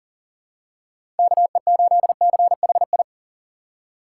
Morse code, heard as a keyed tone, about 40 wpm, 700 Hz. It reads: KE8CHI
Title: USA Call Signs - 6 Characters - 3x Character Spacing - 40wpm